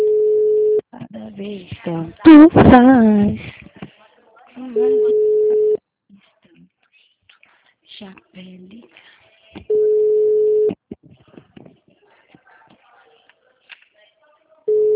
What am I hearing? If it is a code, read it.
Ringback tone (Europe, 425 Hz), with speech.
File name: out- -2023- - - .79778